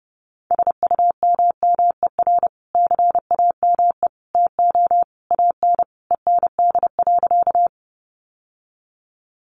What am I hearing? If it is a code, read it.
Morse code, heard as a keyed tone, about 28 wpm, 700 Hz. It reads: SUMMER CAME TO AN END.
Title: Sentences from Top 1100 Words 30wpm